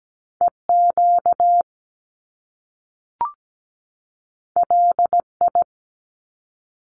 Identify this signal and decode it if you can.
Morse code, heard as a keyed tone, about 17 wpm, 700 Hz. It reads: EQ LI
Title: Top Two Letter Combinations in Words 17wpm